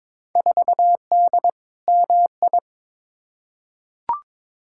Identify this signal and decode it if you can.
Morse code, heard as a keyed tone, about 21 wpm, 700 Hz. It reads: 4D MI
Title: Field Day Exchanges Encore - 22wpm